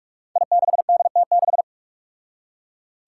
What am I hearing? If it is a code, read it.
Morse code, heard as a keyed tone, about 42 wpm, 700 Hz. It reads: I6DT6
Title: Five Challenging Letter-Number Combinations 45wpm